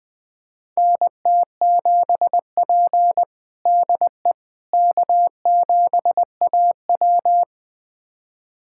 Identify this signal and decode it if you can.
Morse code, heard as a keyed tone, about 20 wpm, 700 Hz. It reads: NT7P DE K7AW
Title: USA Callsign DE Callsign - 4 Characters - 20wpm